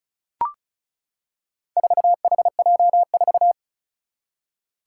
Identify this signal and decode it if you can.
Morse code, heard as a keyed tone, about 36 wpm, 700 Hz. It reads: 4HJ4
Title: Four Challenging Letter-Number Combinations 35wpm